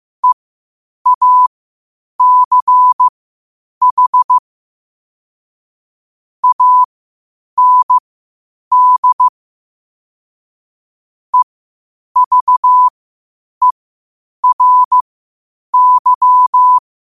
Morse code, heard as a keyed tone, about 15 wpm, spaced as for 8 wpm, 1000 Hz. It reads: EACH AND EVERY
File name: qotdfiles / 8wpm